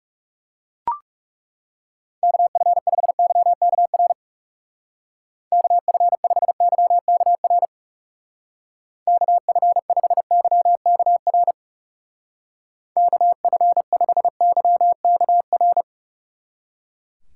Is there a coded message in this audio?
Yes: Morse code from 2.2 to 15.8 s, heard as a keyed tone, about 38 wpm, 700 Hz. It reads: KF5YKR KF5YKR KF5YKR KF5YKR